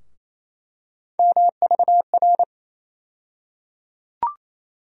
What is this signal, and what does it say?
Morse code, heard as a keyed tone, about 28 wpm, 700 Hz. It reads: MVR
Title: USA Call Sign Postfixes - 3 Letters 28wpm